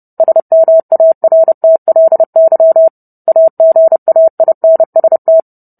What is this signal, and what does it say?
Morse code, heard as a keyed tone, about 30 wpm, 650 Hz. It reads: SMARTLY AGAINST